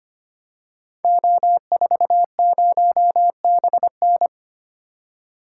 Morse code, heard as a keyed tone, about 25 wpm, 700 Hz. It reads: O40BN
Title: Code Groups - 5 Letter:Numbers - 25wpm